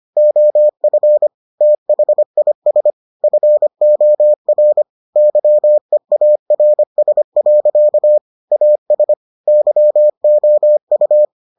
Morse code, heard as a keyed tone, about 25 wpm, 600 Hz. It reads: OF THIS FOR YEARS. AS YOU